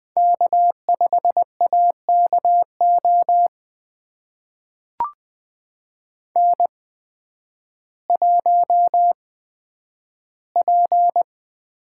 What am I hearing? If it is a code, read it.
Morse code, heard as a keyed tone, about 21 wpm, spaced as for 11 wpm, 700 Hz. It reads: K5AKO N1P
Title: USA Call Signs - 5 Characters - 8x Character Spacing - 20wpm